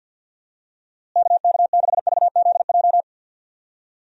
Morse code, heard as a keyed tone, about 47 wpm, 700 Hz. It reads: KK6VZJ
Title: USA Call Signs - 6 Characters - Repeated 2x - 50wpm